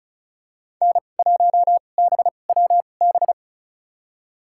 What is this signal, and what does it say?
Morse code, heard as a keyed tone, about 36 wpm, spaced as for 25 wpm, 700 Hz. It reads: N1BWB
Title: USA Call Signs - 5 Characters - 2x Character Spacing - 35wpm